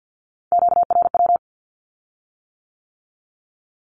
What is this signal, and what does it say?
Morse code, heard as a keyed tone, about 50 wpm, 700 Hz. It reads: QRW
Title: Q-Codes with Spoken Meaning - 50wpm